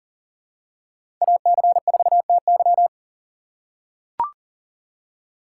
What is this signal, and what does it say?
Morse code, heard as a keyed tone, about 40 wpm, 690 Hz. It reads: AC4TY
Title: USA Call Signs - 5 Characters - 6x Character Spacing - 40wpm